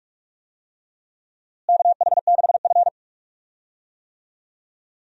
Morse code, heard as a keyed tone, about 46 wpm, 700 Hz. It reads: KH6F
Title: DXCC-100 Call Signs - 4 Characters - Repeated 2x - 45wpm